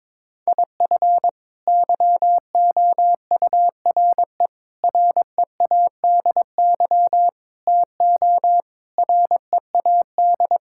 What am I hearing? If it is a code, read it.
Morse code, heard as a keyed tone, about 22 wpm, 700 Hz. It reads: IF YOURE READY TO READ
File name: practice / 22wpm / Sets of 6 Words - 22wpm